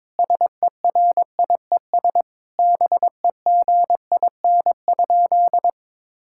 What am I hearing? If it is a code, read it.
Morse code, heard as a keyed tone, about 22 wpm, 700 Hz. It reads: SERIES BEGIN?
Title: Sentences from Top 1000 Words - Review Part 5 - 22wpm